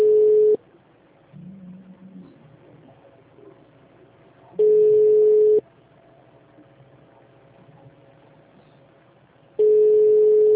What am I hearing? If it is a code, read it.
Ringback tone (Europe, 425 Hz).
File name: out- -2019- - - .21775